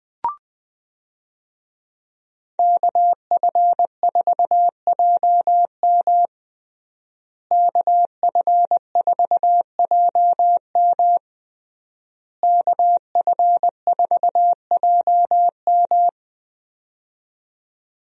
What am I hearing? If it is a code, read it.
Morse code, heard as a keyed tone, about 20 wpm, 700 Hz. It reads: KF4JM KF4JM KF4JM
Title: USA Call Signs - 5 Characters - Repeated 3x - 20wpm